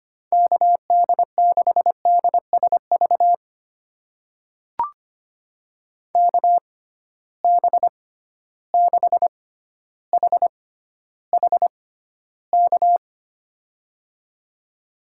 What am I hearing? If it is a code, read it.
Morse code, heard as a keyed tone, about 25 wpm, spaced as for 13 wpm, 700 Hz. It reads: KD6DSV KB6HHK
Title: USA Call Signs - 6 Characters - 6x Character Spacing - 25wpm